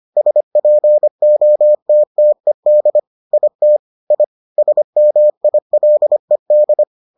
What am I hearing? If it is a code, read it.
Morse code, heard as a keyed tone, about 25 wpm, 600 Hz. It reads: SPOTTED IT I SMILED